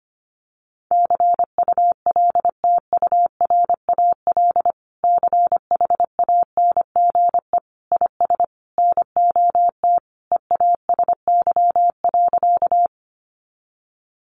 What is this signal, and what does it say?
Morse code, heard as a keyed tone, about 25 wpm, 700 Hz. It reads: CULTURAL CHANGE IS NOT EASY.
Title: Sentences from Top 1500 Words 25wpm